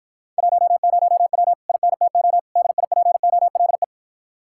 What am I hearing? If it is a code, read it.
Morse code, heard as a keyed tone, about 52 wpm, 700 Hz. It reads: 10W INTO DIPOLE